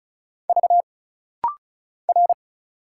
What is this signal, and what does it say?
Morse code, heard as a keyed tone, about 36 wpm, 700 Hz. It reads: V R
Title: Single Letters - Rapid-Fire 35wpm